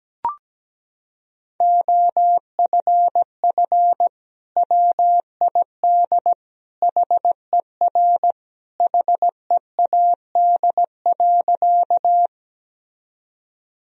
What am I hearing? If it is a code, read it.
Morse code, heard as a keyed tone, about 17 wpm, 700 Hz. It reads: OFF WID HER HEAD.
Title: Sentences from Top 300 Words and Abbreviations 17wpm